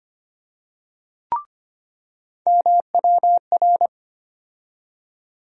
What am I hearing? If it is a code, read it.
Morse code, heard as a keyed tone, about 25 wpm, 700 Hz. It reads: MWR
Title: USA Call Sign Postfixes - 3 Letters 25wpm